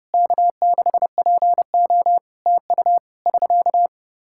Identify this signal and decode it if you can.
Morse code, heard as a keyed tone, about 32 wpm, 700 Hz. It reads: K6PO TU <SK>
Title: Practice QSO - Exchange 4 - Ending - 30wpm